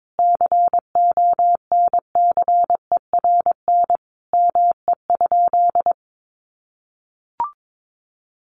Morse code, heard as a keyed tone, about 22 wpm, 700 Hz. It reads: CONCERN ME?
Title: Sentences from Top 1300 Words 22wpm